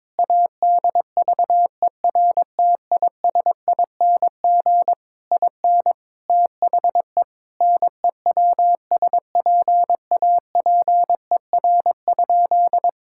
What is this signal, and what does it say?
Morse code, heard as a keyed tone, about 22 wpm, 700 Hz. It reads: ADVERTISING IN THE NEWSPAPER?
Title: Sentences from Top 1000 Words - Review Part 6 - 22wpm